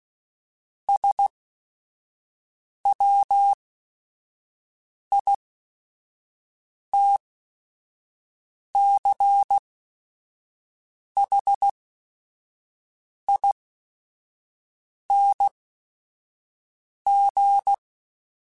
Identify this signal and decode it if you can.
Morse code, heard as a keyed tone, about 16 wpm, spaced as for 5 wpm, 800 Hz. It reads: SWITCHING
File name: Technical Subjects / Technical Subjects-mp3s / Sw Pwr Sup @5